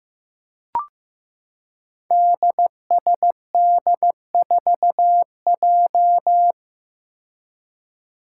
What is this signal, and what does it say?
Morse code, heard as a keyed tone, about 15 wpm, 700 Hz. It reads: DSD4J
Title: Five Challenging Letter-Number Combinations 15wpm